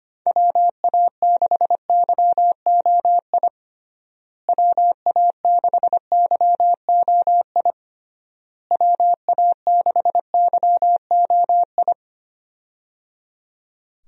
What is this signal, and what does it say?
Morse code, heard as a keyed tone, about 25 wpm, 700 Hz. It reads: WA6YOI WA6YOI WA6YOI
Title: USA Call Signs - 6 Characters - Repeated 3x - 25wpm